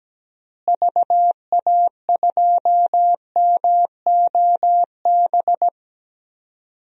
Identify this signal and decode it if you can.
Morse code, heard as a keyed tone, about 17 wpm, 700 Hz. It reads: VA2MOB
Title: DXCC-100 Call Signs - 6 Characters - Repeated 3x - 17wpm